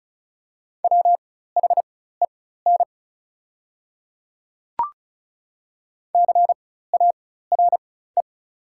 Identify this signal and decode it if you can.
Morse code, heard as a keyed tone, about 36 wpm, spaced as for 16 wpm, 690 Hz. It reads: WHEN CARE
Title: 4x Character Spacing - Top 300 Words - 35wpm